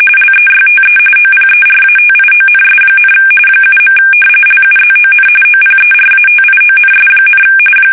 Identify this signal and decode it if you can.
RTTY, about 45.45 baud. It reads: LEYTE OPERATIONS, 7. LUZON OPERATIONS,
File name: fsk850 / battleshipnj